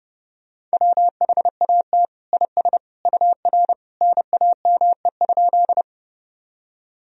Morse code, heard as a keyed tone, about 32 wpm, 700 Hz. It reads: WHAT IS UR NAME?
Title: Sentences from Top 200 Words and Abbreviations 30wpm